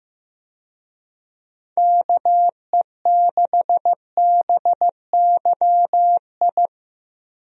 Morse code, heard as a keyed tone, about 15 wpm, 700 Hz. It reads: KE6BYI
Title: USA Call Signs - 6 Characters - Repeated 3x - 15wpm